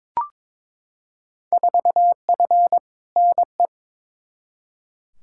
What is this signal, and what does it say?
Morse code, heard as a keyed tone, about 22 wpm, 700 Hz. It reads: 4F NE
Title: Field Day Exchanges Encore - 22wpm